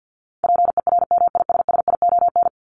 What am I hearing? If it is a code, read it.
Morse code, heard as a keyed tone, about 50 wpm, 700 Hz. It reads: PERMISSION
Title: Top 1501-1750 Words 50wpm